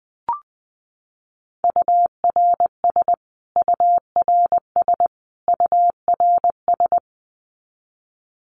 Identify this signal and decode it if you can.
Morse code, heard as a keyed tone, about 19 wpm, 700 Hz. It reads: URS URS URS
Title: Abbreviations Repeated 3x - 20wpm